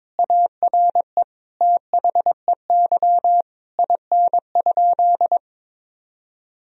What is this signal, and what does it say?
Morse code, heard as a keyed tone, about 22 wpm, 700 Hz. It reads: ARE THEY IN?
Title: Sentences from Top 1000 Words - Review Part 2 - 22wpm